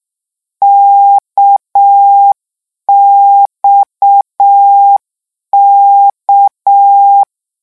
Morse code, heard as a keyed tone, about 6 wpm, 800 Hz. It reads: KXK
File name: lecon 1-vitesse 6